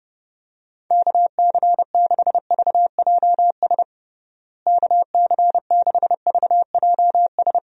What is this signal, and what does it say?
Morse code, heard as a keyed tone, about 32 wpm, 700 Hz. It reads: KC6VJS KC6VJS
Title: USA Call Signs - 6 Characters - Repeated 2x - 30wpm